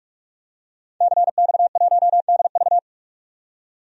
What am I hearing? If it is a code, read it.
Morse code, heard as a keyed tone, about 46 wpm, 700 Hz. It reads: CX1DV
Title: DXCC-100 Call Signs - 5 Characters - 45wpm